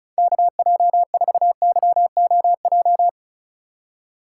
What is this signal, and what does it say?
Morse code, heard as a keyed tone, about 34 wpm, 700 Hz. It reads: KJ4YOJ
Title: USA Call Signs - 6 Characters - 3x Character Spacing - 35wpm